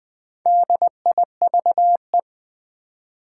Morse code, heard as a keyed tone, about 20 wpm, 700 Hz. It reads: DIVE
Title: Build A Word - 2 Letters At A Time - 4-Letter Words - 20wpm